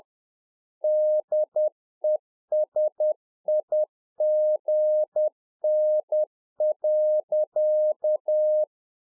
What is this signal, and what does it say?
Morse code, heard as a keyed tone, about 10 wpm, 610 Hz. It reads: DESIGN.
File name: Czech Army Morse Keys 10WPM